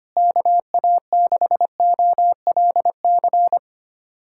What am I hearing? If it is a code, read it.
Morse code, heard as a keyed tone, about 25 wpm, 700 Hz. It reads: KA6OLC
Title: USA Callsign DE Callsign - 6 Characters - 25wpm